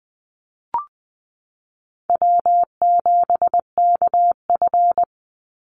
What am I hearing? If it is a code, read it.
Morse code, heard as a keyed tone, about 20 wpm, 700 Hz. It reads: W7KF
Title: Easy Call Signs - 20wpm